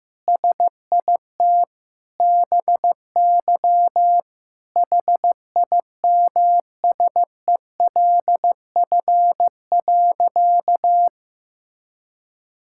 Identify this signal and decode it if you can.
Morse code, heard as a keyed tone, about 15 wpm, 700 Hz. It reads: SIT BY HIMSELF.
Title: Sentences from Top 1600 Words 15wpm